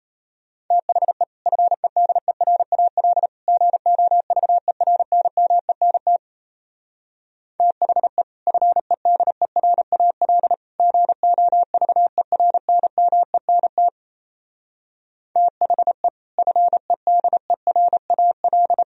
Morse code, heard as a keyed tone, about 34 wpm, 700 Hz. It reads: THE FEDERAL GOVERNMENT THE FEDERAL GOVERNMENT THE FEDERAL
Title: Speed-Racing - Sets of 3 Words 28wpm